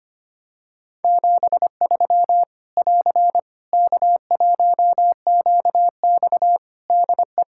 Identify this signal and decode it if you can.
Morse code, heard as a keyed tone, about 25 wpm, 700 Hz. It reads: 73 <AR> K1QX DE
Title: Practice QSO - Exchange 4 - Ending - 25wpm